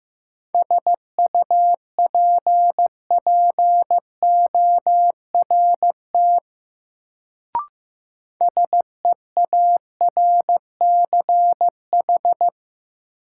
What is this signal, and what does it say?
Morse code, heard as a keyed tone, about 15 wpm, 700 Hz. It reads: SUPPORT SEARCH
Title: Sentences from Top 800 Words - New Word Drill - 15wpm